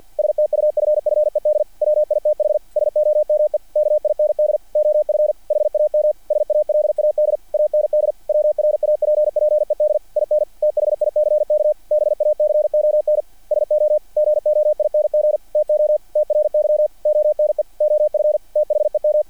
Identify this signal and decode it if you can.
Morse code, heard as a keyed tone, about 48 wpm, 600 Hz. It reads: STUFFED WITH SOME KIND OF HAM SALAD AND WRAPPED IN THICK BACON SO GOING TO TRY ONE OF THEM
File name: archived / CRSnet